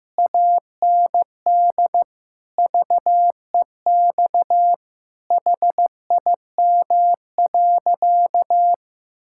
Morse code, heard as a keyed tone, about 16 wpm, 700 Hz. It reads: AND VEX HIM.
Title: Pangram Challenge v2 - 15wpm